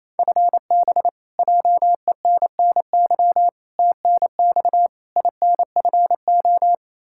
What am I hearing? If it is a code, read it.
Morse code, heard as a keyed tone, about 28 wpm, 700 Hz. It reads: FB JENNY TNX INFO
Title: K1USN SST QSO Practice 28wpm